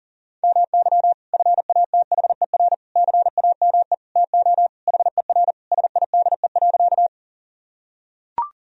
Morse code, heard as a keyed tone, about 40 wpm, 700 Hz. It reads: MY FATHER CAME TO HER SIDE.
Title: Sentences from Top 1000 Words - Review Part 1 - 40wpm